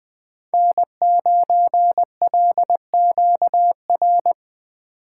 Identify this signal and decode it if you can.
Morse code, heard as a keyed tone, about 20 wpm, 700 Hz. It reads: N9LQR
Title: USA Call Signs - 3x Character Spacing - 20wpm